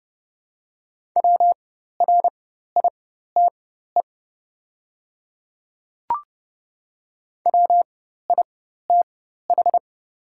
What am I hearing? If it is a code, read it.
Morse code, heard as a keyed tone, about 31 wpm, spaced as for 14 wpm, 700 Hz. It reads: WRITE WITH
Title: 4x Character Spacing - Top 100 Words - 30wpm